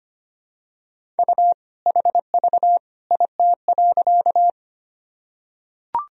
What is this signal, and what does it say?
Morse code, heard as a keyed tone, about 25 wpm, 700 Hz. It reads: U HV IT.
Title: Sentences from Top 100 Words and Abbreviations 25wpm